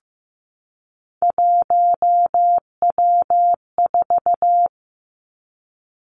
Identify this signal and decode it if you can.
Morse code, heard as a keyed tone, about 15 wpm, 700 Hz. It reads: 1W4
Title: Three Challenging Letter-Number Combinations 15wpm